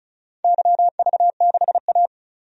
Morse code, heard as a keyed tone, about 36 wpm, 700 Hz. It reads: YV6A
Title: DXCC-100 Call Signs - 4 Characters - 35wpm